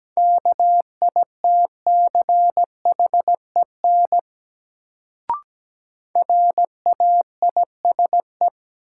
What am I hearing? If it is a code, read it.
Morse code, heard as a keyed tone, about 17 wpm, 700 Hz. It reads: KITCHEN RAISE
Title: Sentences from Top 1400 Words - New Word Drill Alt - 17wpm